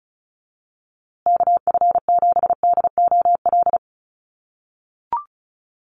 Morse code, heard as a keyed tone, about 34 wpm, 700 Hz. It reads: KF7DOL
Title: USA Call Signs Repeated 2x 35wpm